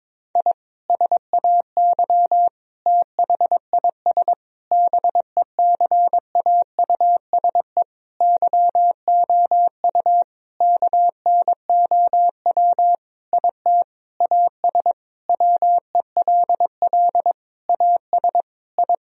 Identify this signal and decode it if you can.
Morse code, heard as a keyed tone, about 22 wpm, 700 Hz. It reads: I SAY THIS BECAUSE YOU KNOW IT AS WELL AS I
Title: Sentences from Top 1000 Words - Review Part 3 - 22wpm